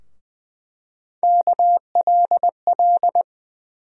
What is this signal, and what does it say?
Morse code, heard as a keyed tone, about 20 wpm, 700 Hz. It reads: KLL
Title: USA Call Sign Postfixes - 3 Letters 20wpm